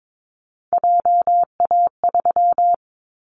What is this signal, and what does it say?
Morse code, heard as a keyed tone, about 22 wpm, 700 Hz. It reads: JA3
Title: Callsign Prefixes for Top 50 DXCC - 22wpm